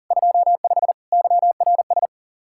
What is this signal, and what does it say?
Morse code, heard as a keyed tone, about 52 wpm, 700 Hz. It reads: 25 YRS